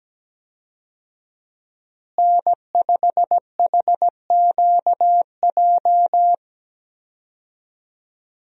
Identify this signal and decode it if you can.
Morse code, heard as a keyed tone, about 17 wpm, 700 Hz. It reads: N5HQJ
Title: USA Call Signs - 5 Characters - Repeated 3x - 17wpm